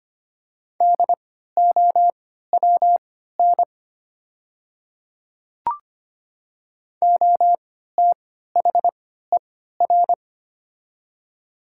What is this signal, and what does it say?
Morse code, heard as a keyed tone, about 25 wpm, spaced as for 14 wpm, 700 Hz. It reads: DOWN OTHER
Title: 3x Character Spacing - Top 100 Words - 25wpm